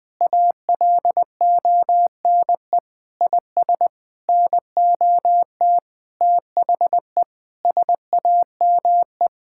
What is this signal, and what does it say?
Morse code, heard as a keyed tone, about 20 wpm, 700 Hz. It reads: ALONE IS NOT THE SAME
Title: Sentences with 6-Word Phrases - 20wpm